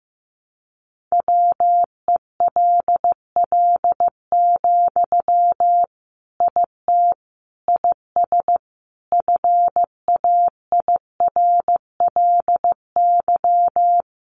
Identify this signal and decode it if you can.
Morse code, heard as a keyed tone, about 14 wpm, 700 Hz. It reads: WELL, IT IS FAIRLY
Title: Sentences from Top 1100 Words 15wpm